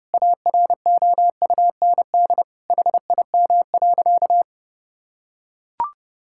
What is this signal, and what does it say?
Morse code, heard as a keyed tone, about 30 wpm, 700 Hz. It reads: AROUND HIM.
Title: Sentences from Top 1000 Words - Review Part 6 - 30wpm